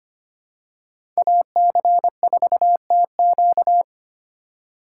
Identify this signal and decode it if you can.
Morse code, heard as a keyed tone, about 25 wpm, 690 Hz. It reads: AC4TQ
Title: USA Call Signs - 4x Character Spacing - 25wpm